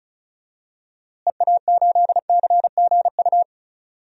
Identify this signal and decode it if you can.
Morse code, heard as a keyed tone, about 34 wpm, 700 Hz. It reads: EA8CGU
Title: DXCC-100 Call Signs - 6 Characters - Repeated 2x - 35wpm